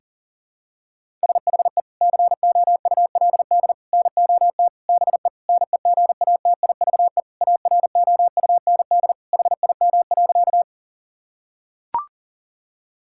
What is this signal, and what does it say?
Morse code, heard as a keyed tone, about 40 wpm, 700 Hz. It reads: SHE COULD NOT BE NEGATIVE AROUND HIM.
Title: Sentences from Top 1000 Words - Review Part 6 - 40wpm